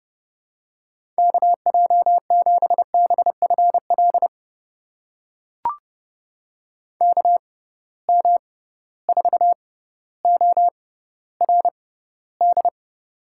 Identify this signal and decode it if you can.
Morse code, heard as a keyed tone, about 31 wpm, spaced as for 15 wpm, 700 Hz. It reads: KJ7BFL KM4ORD
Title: USA Call Signs - 6 Characters - 6x Character Spacing - 30wpm